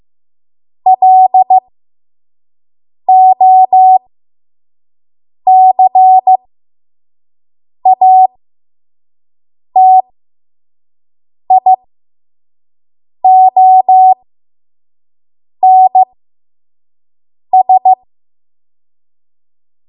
Morse code, heard as a keyed tone, about 15 wpm, spaced as for 5 wpm, 750 Hz. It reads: LOCATIONS
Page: 5 WPM Code Archive